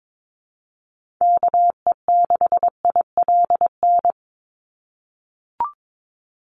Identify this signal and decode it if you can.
Morse code, heard as a keyed tone, about 22 wpm, 700 Hz. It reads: KE6ILN
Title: USA Call Signs 22wpm